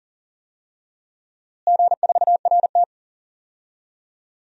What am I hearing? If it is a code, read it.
Morse code, heard as a keyed tone, about 40 wpm, 700 Hz. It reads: G4RT
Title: DXCC-100 Call Signs - 4 Characters - Repeated 2x - 40wpm